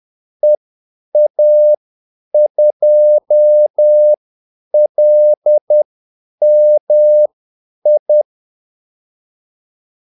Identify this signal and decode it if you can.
Morse code, heard as a keyed tone, about 10 wpm, spaced as for 8 wpm, 600 Hz. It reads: EA2LMI
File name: Callsigns-10-8wpm